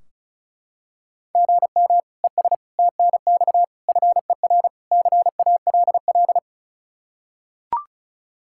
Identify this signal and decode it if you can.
Morse code, heard as a keyed tone, about 36 wpm, 700 Hz. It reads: GM ES TNX FER CALL
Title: Practice QSO - Exchange 1 - Beginning - 35wpm